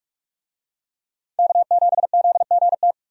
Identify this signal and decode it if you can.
Morse code, heard as a keyed tone, about 43 wpm, 700 Hz. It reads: K7ZGT
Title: USA Call Signs - 5 Characters - 45wpm